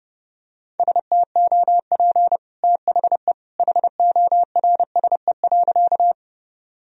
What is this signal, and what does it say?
Morse code, heard as a keyed tone, about 30 wpm, 700 Hz. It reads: STOP THE HORSE.
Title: Sentences from Top 1000 Words - Review Part 1 - 30wpm